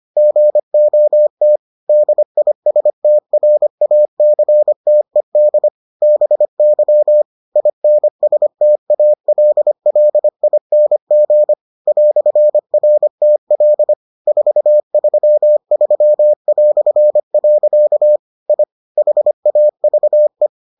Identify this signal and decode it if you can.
Morse code, heard as a keyed tone, about 25 wpm, 600 Hz. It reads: GOT DISTRACTED BY INSTALLING "RTL 433". I HAVE